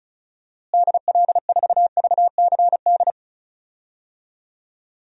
Morse code, heard as a keyed tone, about 36 wpm, 700 Hz. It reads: DL4VCD
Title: DXCC-100 Call Signs - 6 Characters - Repeated 2x - 35wpm